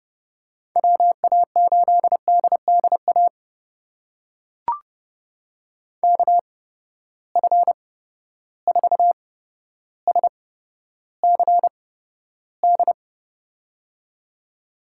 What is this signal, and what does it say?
Morse code, heard as a keyed tone, about 30 wpm, spaced as for 13 wpm, 700 Hz. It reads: WA8DDA KF4SCD
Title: USA Call Signs - 8x Character Spacing - 30wpm